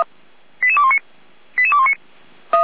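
Five-tone selective call (ZVEI), heard as quick runs of five tones, tones of about 70 ms. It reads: 88219 88219
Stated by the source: ZVEI five-tone call 88329 received by SDR.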